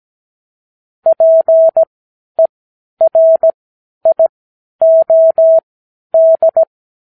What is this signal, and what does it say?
Morse code, heard as a keyed tone, about 17 wpm, spaced as for 10 wpm, 650 Hz. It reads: PERIOD
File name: k7qo / Lesson 59 Period